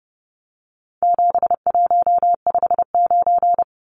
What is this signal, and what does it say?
Morse code, heard as a keyed tone, about 30 wpm, 700 Hz. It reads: 7159
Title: Numbers - 4-Digit 30wpm